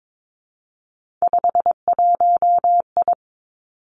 Morse code, heard as a keyed tone, about 22 wpm, 700 Hz. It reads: 51I
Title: Three Challenging Letter-Number Combinations 22wpm